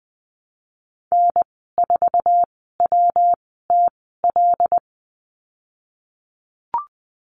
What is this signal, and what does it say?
Morse code, heard as a keyed tone, about 20 wpm, spaced as for 15 wpm, 700 Hz. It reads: N4WTL
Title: 140 Lesson 23g - Letter f - Call Signs 20wpm